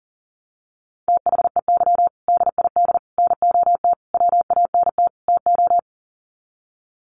Morse code, heard as a keyed tone, about 37 wpm, 700 Hz. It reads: THEY DID NOT WANT TO
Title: Sets of 5 Words 40wpm